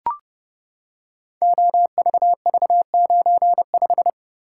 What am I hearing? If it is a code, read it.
Morse code, heard as a keyed tone, about 28 wpm, 700 Hz. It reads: OVV95
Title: Code Groups - 5 Letter:Numbers - 30wpm